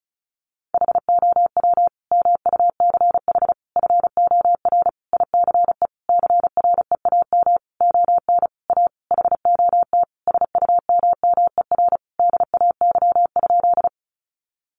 Morse code, heard as a keyed tone, about 34 wpm, 700 Hz. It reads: HOW MUCH FOR ICE CREAM ON A HOT SUMMER DAY?
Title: Sentences from Top 1500 Words 35wpm